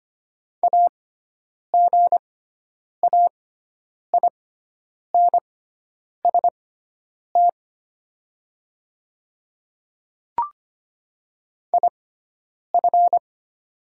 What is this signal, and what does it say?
Morse code, heard as a keyed tone, about 25 wpm, spaced as for 9 wpm, 700 Hz. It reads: AGAINST IF
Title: 6x Character Spacing - Top 500 Words - 25wpm